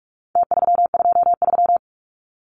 Morse code, heard as a keyed tone, about 44 wpm, 700 Hz. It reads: T323
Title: Numbers - Contest Style - 4-Digit 45wpm